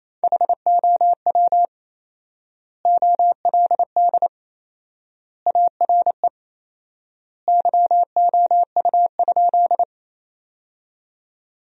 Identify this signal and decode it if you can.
Morse code, heard as a keyed tone, about 27 wpm, 700 Hz. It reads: HOW OLD ARE YOU?
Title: 4x Word-Spacing - Sentences from Top 200 Words - 28wpm